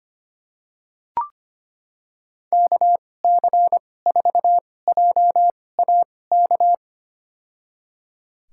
Morse code, heard as a keyed tone, about 25 wpm, spaced as for 18 wpm, 700 Hz. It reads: KC4JAK
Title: USA Call Signs - 6 Characters - 2x Character Spacing - 25wpm